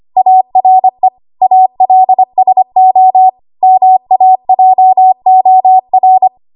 Morse code, heard as a keyed tone, about 25 wpm, 750 Hz. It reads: ARE ALSO MAJOR
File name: Archive / 25 WPM / 25WPM